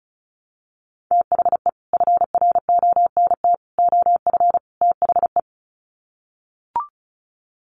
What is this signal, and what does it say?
Morse code, heard as a keyed tone, about 34 wpm, 700 Hz. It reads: THE FRONT OF THE
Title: Sets of 4 Words 35wpm